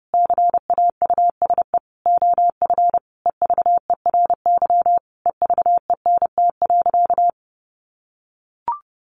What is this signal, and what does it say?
Morse code, heard as a keyed tone, about 30 wpm, 700 Hz. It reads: CAUSE OF EVERY EVENT.
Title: Sentences from Top 1400 Words 30wpm